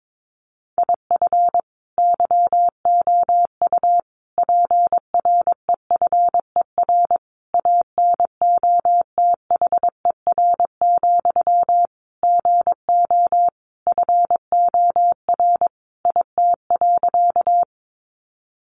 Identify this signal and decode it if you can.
Morse code, heard as a keyed tone, about 22 wpm, 700 Hz. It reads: IF YOU PREFER ANOTHER, GO FOR IT.
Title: Sentences from Top 1100 Words 22wpm